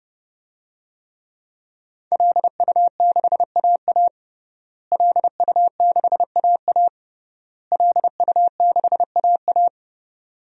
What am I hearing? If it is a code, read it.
Morse code, heard as a keyed tone, about 30 wpm, 700 Hz. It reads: LU6AA LU6AA LU6AA
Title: DXCC-100 Call Signs - 5 Characters - Repeated 3x - 30wpm